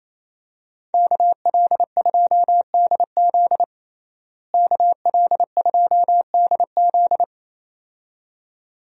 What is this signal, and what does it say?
Morse code, heard as a keyed tone, about 28 wpm, 700 Hz. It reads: KL2DZ KL2DZ
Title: USA Call Signs - 5 Characters - Repeated 2x - 28wpm